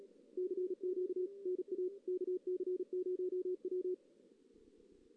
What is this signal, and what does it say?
Morse code, heard as a keyed tone, about 36 wpm, 370 Hz. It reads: CQ NA KC0W
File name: 2016 JAN CW NAQP / 2016 Jan naqp k5ot - 0-2